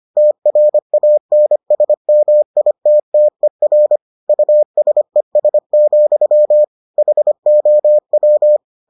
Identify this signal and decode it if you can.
Morse code, heard as a keyed tone, about 25 wpm, 600 Hz. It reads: TRANSMITTER USES, HOW